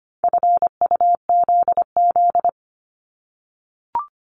Morse code, heard as a keyed tone, about 25 wpm, 700 Hz. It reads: FUZZ
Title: Multi-letter Phonemes 25wpm